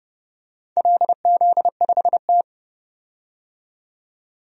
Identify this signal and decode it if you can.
Morse code, heard as a keyed tone, about 31 wpm, 700 Hz. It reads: LZ5T
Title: DXCC-100 Call Signs - 4 Characters - 30wpm